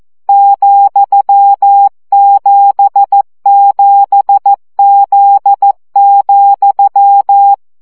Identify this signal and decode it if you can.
Morse code, heard as a keyed tone, about 14 wpm, 800 Hz. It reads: ,77Z,
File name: CW gyakorlas z 7 vesszo 12 baud